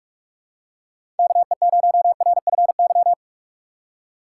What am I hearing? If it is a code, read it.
Morse code, heard as a keyed tone, about 46 wpm, 700 Hz. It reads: KE0RFY
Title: USA Call Signs - 6 Characters - 2x Character Spacing - 45wpm